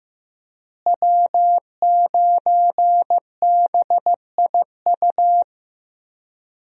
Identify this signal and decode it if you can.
Morse code, heard as a keyed tone, about 15 wpm, 700 Hz. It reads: W9BIU
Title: USA Call Signs - 5 Characters - Repeated 3x - 15wpm